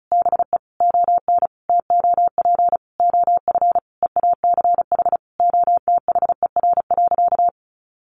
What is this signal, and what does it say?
Morse code, heard as a keyed tone, about 34 wpm, 700 Hz. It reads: BE ON TOP OF EACH OTHER.
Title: Sentences from Top 700 Words 35wpm